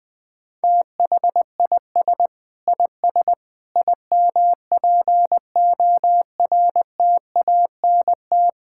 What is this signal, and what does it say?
Morse code, heard as a keyed tone, about 20 wpm, 700 Hz. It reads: THIS IS IMPORTANT
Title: Sentences with 5-Word Phrases - 20wpm